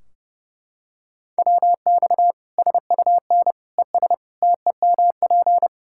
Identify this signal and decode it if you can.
Morse code, heard as a keyed tone, about 30 wpm, 700 Hz. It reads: WX SUN ES TEMP
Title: Practice QSO - Exchange 2 - Info - 30wpm